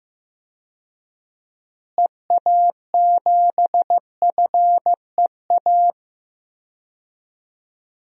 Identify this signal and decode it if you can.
Morse code, heard as a keyed tone, about 16 wpm, 700 Hz. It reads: EA7FEA
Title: DXCC-100 Call Signs - 6 Characters - Repeated 3x - 15wpm